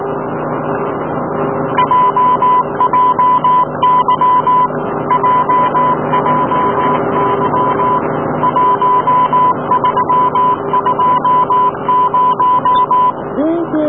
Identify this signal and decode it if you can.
Morse code, heard as a keyed tone, about 18 wpm, 1000 Hz, with radio noise. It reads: JJY JJY 1320
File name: snj roseDB / jjy